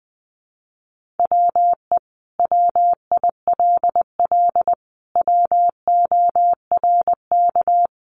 Morse code, heard as a keyed tone, about 20 wpm, 700 Hz. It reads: WE WILL WORK
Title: Sentences from Top 1100 Words 20wpm